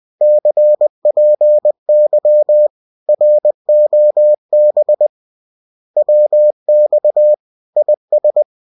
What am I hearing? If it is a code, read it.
Morse code, heard as a keyed tone, about 20 wpm, 600 Hz. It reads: CPY ROB WX IS